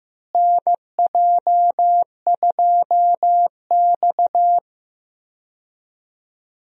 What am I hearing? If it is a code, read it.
Morse code, heard as a keyed tone, about 15 wpm, 700 Hz. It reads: NJ2X
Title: USA Call Signs - 4 Characters - Repeated 3x - 15wpm